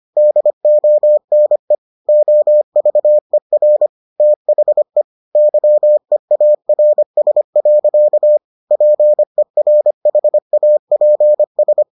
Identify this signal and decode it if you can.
Morse code, heard as a keyed tone, about 25 wpm, 600 Hz. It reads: DONE OVER THE YEARS. PERHAPS